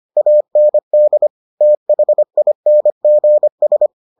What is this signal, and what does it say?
Morse code, heard as a keyed tone, about 25 wpm, 600 Hz. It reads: AND THINGS